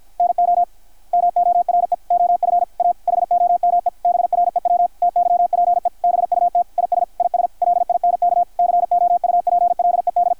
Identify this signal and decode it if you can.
Morse code, heard as a keyed tone, about 52 wpm, 660 Hz, with radio noise. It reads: NO MORE OF A HOME BREW TYPE BUT IS IS LINK COUPLED
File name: archived / CRSnet